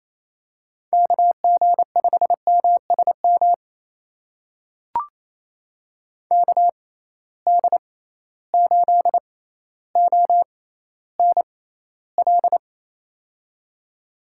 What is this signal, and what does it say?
Morse code, heard as a keyed tone, about 28 wpm, spaced as for 14 wpm, 700 Hz. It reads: KG5MSM KD8ONL
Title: USA Call Signs - 6 Characters - 6x Character Spacing - 28wpm